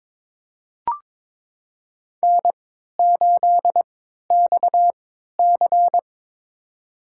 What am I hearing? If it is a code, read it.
Morse code, heard as a keyed tone, about 22 wpm, spaced as for 13 wpm, 700 Hz. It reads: N8XC
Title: USA Call Signs - 4 Characters - 3x Character Spacing - 22wpm